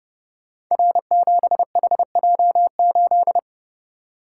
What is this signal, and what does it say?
Morse code, heard as a keyed tone, about 30 wpm, 700 Hz. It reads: R7HJ8
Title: Code Groups - 5 Letter:Numbers - 30wpm